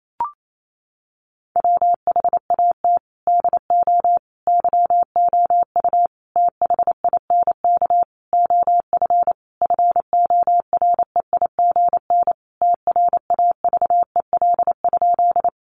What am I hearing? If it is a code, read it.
Morse code, heard as a keyed tone, about 28 wpm, 700 Hz. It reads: WHAT DO YOU THINK OF FOREIGN TRAVEL?
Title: Sentences from Top 1400 Words 28wpm